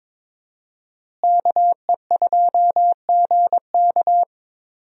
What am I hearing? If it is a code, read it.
Morse code, heard as a keyed tone, about 22 wpm, 700 Hz. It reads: KE2GK
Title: USA Call Signs - 5 Characters - Repeated 3x - 22wpm